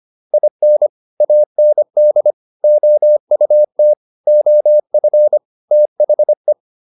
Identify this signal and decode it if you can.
Morse code, heard as a keyed tone, about 25 wpm, 600 Hz. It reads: IN AND OUT OF THE